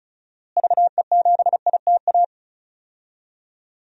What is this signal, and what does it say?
Morse code, heard as a keyed tone, about 36 wpm, 700 Hz. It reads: VE7ITA
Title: DXCC-100 Call Signs - 6 Characters - Repeated 2x - 35wpm